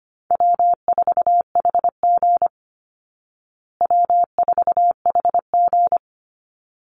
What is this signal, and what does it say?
Morse code, heard as a keyed tone, about 25 wpm, 700 Hz. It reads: W4HG W4HG
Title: Easy Call Signs Repeated 2x - 25wpm